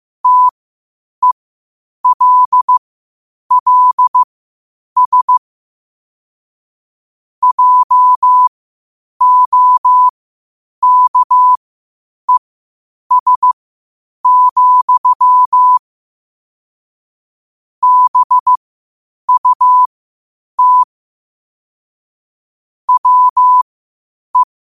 Morse code, heard as a keyed tone, about 15 wpm, spaced as for 8 wpm, 1000 Hz. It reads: TELLS JOKES, BUT WE
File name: qotdfiles / 8wpm